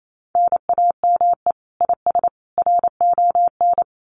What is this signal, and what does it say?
Morse code, heard as a keyed tone, about 28 wpm, 700 Hz. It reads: NAME IS RON
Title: Common QSO Phrases 28wpm